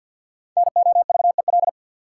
Morse code, heard as a keyed tone, about 53 wpm, 700 Hz. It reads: NOVEL
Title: Sentences from Top 1300 Words - New Word Drill Alt - 50wpm